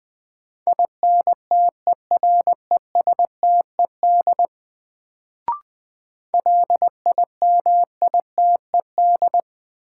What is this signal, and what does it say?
Morse code, heard as a keyed tone, about 20 wpm, 700 Hz. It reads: INTERESTED LIMITED
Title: Sentences from Top 900 Words - New Word Drill - 20wpm